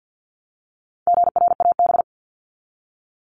Morse code, heard as a keyed tone, about 50 wpm, 700 Hz. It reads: GRAB
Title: Top 1751-2000 Words 50wpm